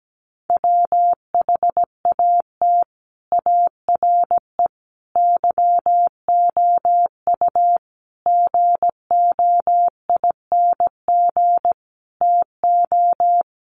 Morse code, heard as a keyed tone, about 17 wpm, 700 Hz. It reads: WHAT ARE YOU GOING TO
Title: Sets of 5 Words 17wpm